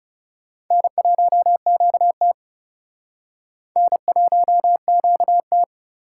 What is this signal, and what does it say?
Morse code, heard as a keyed tone, about 40 wpm, 700 Hz. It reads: N1QT N1QT